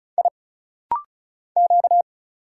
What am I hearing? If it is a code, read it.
Morse code, heard as a keyed tone, about 34 wpm, 700 Hz. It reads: I Q
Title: Single Letter-Number Alt - Rapid-Fire 35wpm